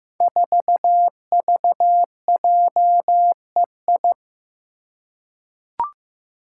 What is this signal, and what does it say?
Morse code, heard as a keyed tone, about 15 wpm, 700 Hz. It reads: 4VJEI
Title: Five Challenging Letter-Number Combinations 15wpm